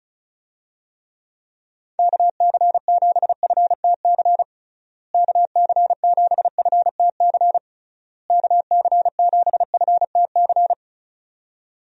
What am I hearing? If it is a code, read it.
Morse code, heard as a keyed tone, about 34 wpm, 700 Hz. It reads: KC7FTC KC7FTC KC7FTC
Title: USA Call Signs - 6 Characters - Repeated 3x - 35wpm